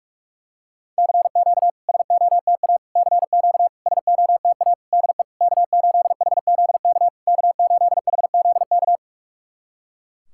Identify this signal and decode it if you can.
Morse code, heard as a keyed tone, about 44 wpm, 700 Hz. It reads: CQ SOTA CQ SOTA DE K8HZK K8HZK
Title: SOTA QSO Practice 45wpm